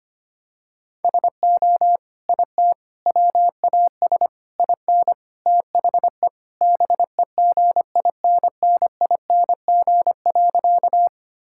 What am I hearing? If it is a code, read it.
Morse code, heard as a keyed tone, about 25 wpm, 700 Hz. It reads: SO IT WAS IN THE BEGINNING.
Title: Sentences from Top 1000 Words - Review Part 5 - 25wpm